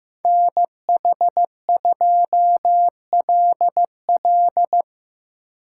Morse code, heard as a keyed tone, about 17 wpm, 700 Hz. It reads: NH2LL
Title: Speed Racing - USA 2x2 Call Signs 15wpm